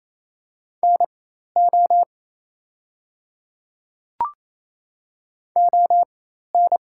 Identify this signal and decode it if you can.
Morse code, heard as a keyed tone, about 28 wpm, spaced as for 13 wpm, 700 Hz. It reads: NO ON
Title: 4x Character Spacing - Top 100 Words - 28wpm